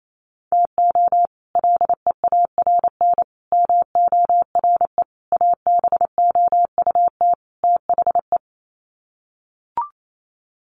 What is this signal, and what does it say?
Morse code, heard as a keyed tone, about 28 wpm, 700 Hz. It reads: TO LEARN MORE ABOUT THE
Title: Sets of 5 Words Encore2 28wpm